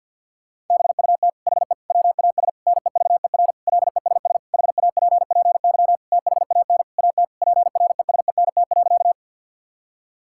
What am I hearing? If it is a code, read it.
Morse code, heard as a keyed tone, about 49 wpm, 690 Hz. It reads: BUT HE WAS NEVER LESS HAPPY THAN AT PRESENT.
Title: Sentences from Top 1000 Words - Review Part 5 - 50wpm